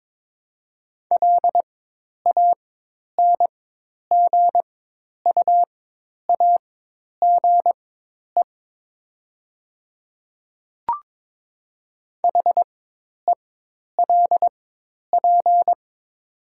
Morse code, heard as a keyed tone, about 22 wpm, spaced as for 10 wpm, 700 Hz. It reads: LANGUAGE HELP
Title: 4x Character Spacing - Top 500 Words - 22wpm